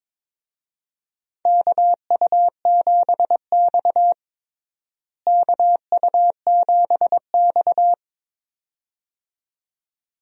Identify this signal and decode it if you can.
Morse code, heard as a keyed tone, about 22 wpm, 700 Hz. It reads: KU7X KU7X
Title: USA Call Signs - 4 Characters - Repeated 2x - 22wpm